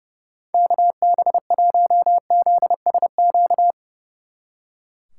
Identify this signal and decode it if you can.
Morse code, heard as a keyed tone, about 36 wpm, 700 Hz. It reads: KB1ZSQ
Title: Speed Racing - USA 2x3 Call Signs 30wpm